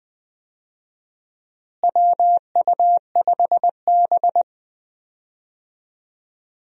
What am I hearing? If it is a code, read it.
Morse code, heard as a keyed tone, about 20 wpm, 700 Hz. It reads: WU5B
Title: USA Call Signs - 4 Characters - Repeated 2x - 20wpm